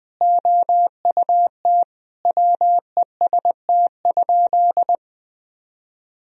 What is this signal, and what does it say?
Morse code, heard as a keyed tone, about 20 wpm, 700 Hz. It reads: OUT WEST?
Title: Sentences from Top 1000 Words - Review Part 2 - 20wpm